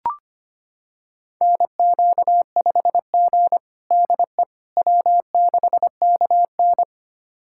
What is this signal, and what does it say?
Morse code, heard as a keyed tone, about 25 wpm, 700 Hz. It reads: NQ5G DE W6KN
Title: USA Callsign DE Callsign - 4 Characters - 25wpm